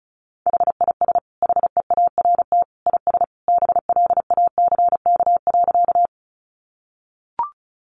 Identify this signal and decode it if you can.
Morse code, heard as a keyed tone, about 34 wpm, 700 Hz. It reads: HIS HEART IS BLACK.
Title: Sentences from Top 1200 Words 35wpm